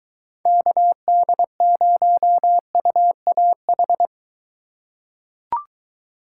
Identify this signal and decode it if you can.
Morse code, heard as a keyed tone, about 23 wpm, 700 Hz. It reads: KD0UAH